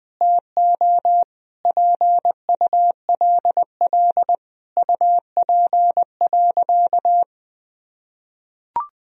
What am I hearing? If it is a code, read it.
Morse code, heard as a keyed tone, about 21 wpm, 700 Hz. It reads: TO PULL UP.
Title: Sentences from Top 1000 Words - Review Part 2 - 20wpm